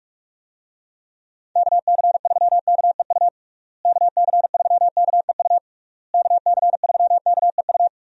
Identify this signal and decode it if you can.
Morse code, heard as a keyed tone, about 46 wpm, 700 Hz. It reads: KC3KEU KC3KEU KC3KEU
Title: USA Call Signs - 6 Characters - Repeated 3x - 45wpm